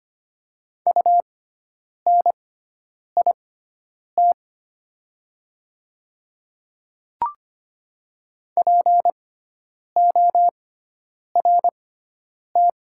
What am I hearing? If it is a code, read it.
Morse code, heard as a keyed tone, about 25 wpm, spaced as for 9 wpm, 700 Hz. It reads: UNIT PORT
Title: 6x Character Spacing - Top 500 Words - 25wpm